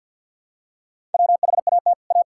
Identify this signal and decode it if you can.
Morse code, heard as a keyed tone, about 50 wpm, 700 Hz. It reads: WHAT A
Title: Sentences from Top 1600 Words 50wpm